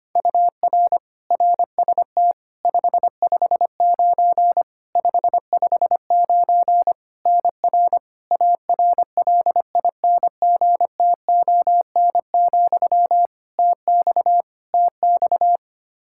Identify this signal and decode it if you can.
Morse code, heard as a keyed tone, about 25 wpm, 700 Hz. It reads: UR RST 559 559 NR ARLINGTON, TX TX
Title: Practice QSO - Exchange 1 - Beginning - 25wpm